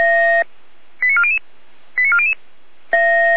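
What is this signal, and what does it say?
Five-tone selective call (ZVEI), heard as quick runs of five tones, tones of about 70 ms. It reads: 89400 89400
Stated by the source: ZVEI five-tone selective call received on an SDR by the Feuerwehr Orlamünde fire brigade.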